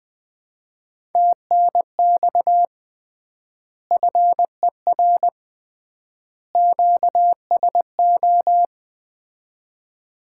Morse code, heard as a keyed tone, about 20 wpm, 700 Hz. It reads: TNX FER QSO
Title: Common QSO Phrases - 3x Word-Spacing - 20wpm